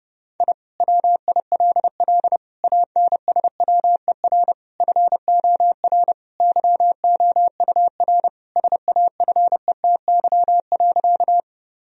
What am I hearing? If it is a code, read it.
Morse code, heard as a keyed tone, about 30 wpm, 700 Hz. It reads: I WILL ANSWER FOR YOUR SAFETY.
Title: Sentences from Top 1000 Words - Review Part 4 - 30wpm